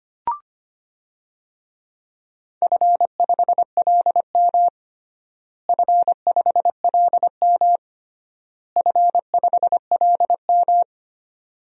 Morse code, heard as a keyed tone, about 25 wpm, 700 Hz. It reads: F5LM F5LM F5LM
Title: DXCC-100 Call Signs - 4 Characters - Repeated 3x - 25wpm